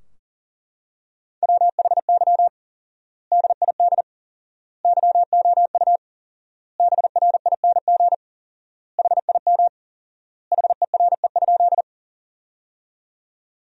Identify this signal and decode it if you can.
Morse code, heard as a keyed tone, about 40 wpm, 700 Hz. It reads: WHY DID YOU BRING HIM HERE?
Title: 4x Word-Spacing - Sentences from Top 500 Words - 40wpm